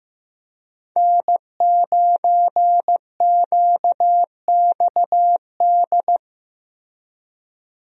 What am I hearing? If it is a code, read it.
Morse code, heard as a keyed tone, about 15 wpm, 700 Hz. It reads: N9QXD
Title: USA Call Signs - 5 Characters - Repeated 2x - 15wpm